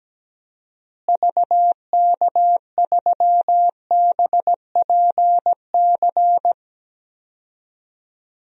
Morse code, heard as a keyed tone, about 17 wpm, 700 Hz. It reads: VK3BPC
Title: DXCC-100 Call Signs - 6 Characters - Repeated 3x - 17wpm